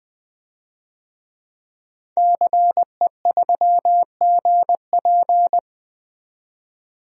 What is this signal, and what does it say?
Morse code, heard as a keyed tone, about 20 wpm, 700 Hz. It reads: CE3GP
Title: DXCC-100 Call Signs - 5 Characters - 20wpm